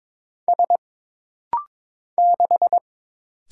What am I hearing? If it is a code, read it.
Morse code, heard as a keyed tone, about 22 wpm, 700 Hz. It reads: S 6
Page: This rapid-fire timing is geared to push your ICR (Instant Character Recognition) skills.